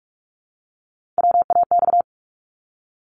Morse code, heard as a keyed tone, about 46 wpm, 700 Hz. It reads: WAX
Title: Multi-letter Phonemes Repeated 3x 45wpm